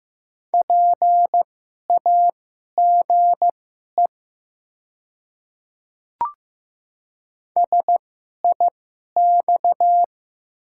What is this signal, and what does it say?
Morse code, heard as a keyed tone, about 15 wpm, spaced as for 11 wpm, 700 Hz. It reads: PAGE SIX
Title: 2x Character Spacing - Top 500 Words - 15wpm